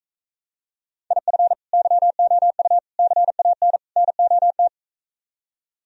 Morse code, heard as a keyed tone, about 44 wpm, 700 Hz. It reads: IF YOU CAN NOT